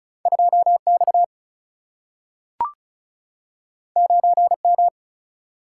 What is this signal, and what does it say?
Morse code, heard as a keyed tone, about 36 wpm, 700 Hz. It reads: 2 X 9 M
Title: Single-Letter Single-Number 35wpm